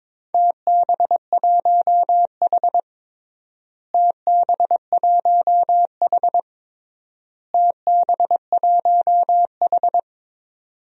Morse code, heard as a keyed tone, about 22 wpm, 700 Hz. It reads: TB1H TB1H TB1H
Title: DXCC-100 Call Signs - 4 Characters - Repeated 3x - 22wpm